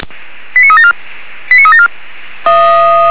Five-tone selective call (ZVEI), heard as quick runs of five tones, tones of about 70 ms. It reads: 89374 89374